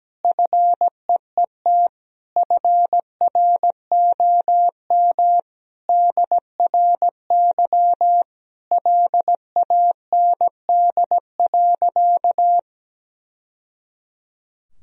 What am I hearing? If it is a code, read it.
Morse code, heard as a keyed tone, about 17 wpm, 700 Hz. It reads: FEET FROM DRY LAND.
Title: Sentences from Top 1000 Words - Review Part 3 - 17wpm